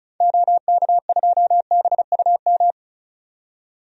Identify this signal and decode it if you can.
Morse code, heard as a keyed tone, about 36 wpm, 700 Hz. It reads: OK2BUM
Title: DXCC-100 Call Signs - 6 Characters - 35wpm